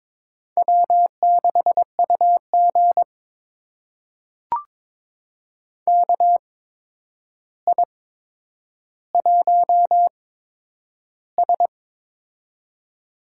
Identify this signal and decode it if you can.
Morse code, heard as a keyed tone, about 22 wpm, spaced as for 9 wpm, 700 Hz. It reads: W6UG KI1S
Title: USA Call Signs - 4 Characters - 8x Character Spacing - 22wpm